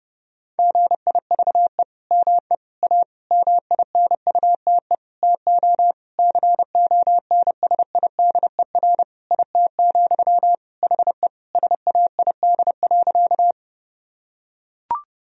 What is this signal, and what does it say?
Morse code, heard as a keyed tone, about 30 wpm, 700 Hz. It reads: GIVE ME A MINUTE TO CONSIDER IT, HE SAID.
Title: Sentences from Top 1000 Words - Review Part 4 - 30wpm